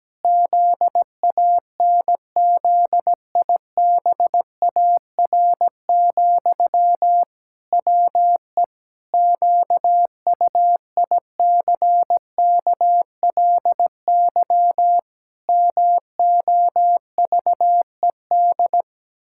Morse code, heard as a keyed tone, about 17 wpm, 700 Hz. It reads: ZANZIBAR, WE QUICKLY MOVED
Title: Pangram Challenge v2 - 17wpm